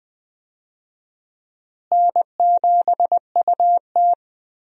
Morse code, heard as a keyed tone, about 20 wpm, 700 Hz. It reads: N7UT
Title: USA Call Signs - 4 Characters - Repeated 2x - 20wpm